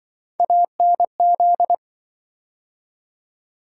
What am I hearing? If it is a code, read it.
Morse code, heard as a keyed tone, about 28 wpm, 700 Hz. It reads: ANZ